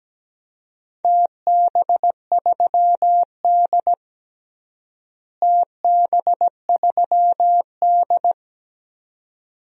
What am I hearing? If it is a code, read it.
Morse code, heard as a keyed tone, about 17 wpm, 700 Hz. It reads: TB3D TB3D
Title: DXCC-100 Call Signs - 4 Characters - Repeated 2x - 17wpm